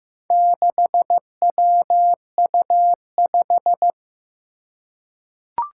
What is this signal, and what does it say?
Morse code, heard as a keyed tone, about 15 wpm, 700 Hz. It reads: 6WU5
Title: Four Challenging Letter-Number Combinations 15wpm